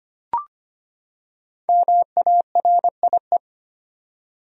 Morse code, heard as a keyed tone, about 25 wpm, 700 Hz. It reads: MARIE
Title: USA Common Names - 25wpm